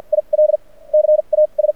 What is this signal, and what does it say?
Morse code, heard as a keyed tone, about 52 wpm, spaced as for 31 wpm, 590 Hz. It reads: ILCAS
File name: archived / CRSnet